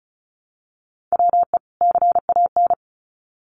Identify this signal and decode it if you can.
Morse code, heard as a keyed tone, about 34 wpm, 700 Hz. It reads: WE CAN
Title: Sets of 2 Words 35wpm